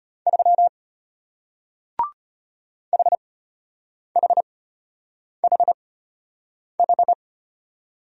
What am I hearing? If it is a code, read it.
Morse code, heard as a keyed tone, about 34 wpm, spaced as for 8 wpm, 700 Hz. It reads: 3 HHHH